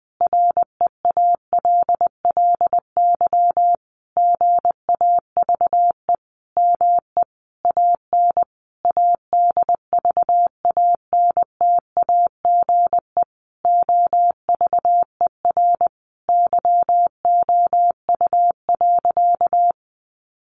Morse code, heard as a keyed tone, about 20 wpm, 700 Hz. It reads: REALLY GAVE ME AN ADVANTAGE OVER YOU.
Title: Sentences from Top 900 Words 20wpm